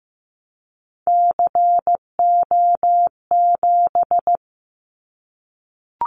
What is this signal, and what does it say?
Morse code, heard as a keyed tone, about 15 wpm, 700 Hz. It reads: CO7
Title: Callsign Prefixes for Top 100 DXCC - 15wpm